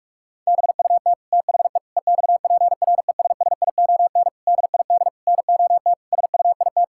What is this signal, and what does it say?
Morse code, heard as a keyed tone, about 46 wpm, 700 Hz. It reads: BUT THE EXPRESSION DID NOT SUIT
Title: Sentences from Top 1600 Words 45wpm